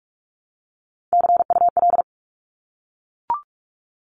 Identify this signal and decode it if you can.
Morse code, heard as a keyed tone, about 44 wpm, 700 Hz. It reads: CUL
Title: Abbreviations 45wpm